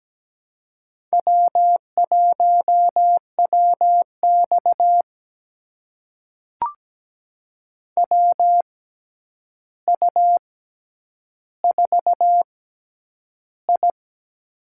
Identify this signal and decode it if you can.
Morse code, heard as a keyed tone, about 17 wpm, spaced as for 9 wpm, 700 Hz. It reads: W1WX WU4I
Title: USA Call Signs - 4 Characters - 6x Character Spacing - 17wpm